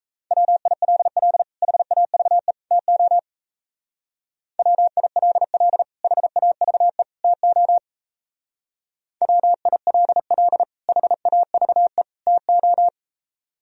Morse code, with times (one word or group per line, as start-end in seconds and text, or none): 0.3-1.4 WILL
1.6-2.5 HAVE
2.7-3.2 TO
4.6-5.8 WILL
6.0-7.0 HAVE
7.2-7.8 TO
9.2-10.6 WILL
10.9-12.0 HAVE
12.3-12.9 TO